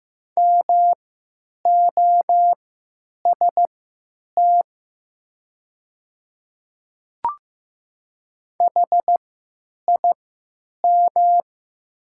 Morse code, heard as a keyed tone, about 15 wpm, spaced as for 9 wpm, 700 Hz. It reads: MOST HIM
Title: 3x Character Spacing - Top 100 Words - 15wpm